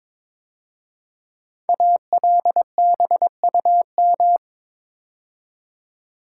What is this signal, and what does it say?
Morse code, heard as a keyed tone, about 22 wpm, 700 Hz. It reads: ALBUM
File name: 5 Letter Words - First 3 Letters Given - 22wpm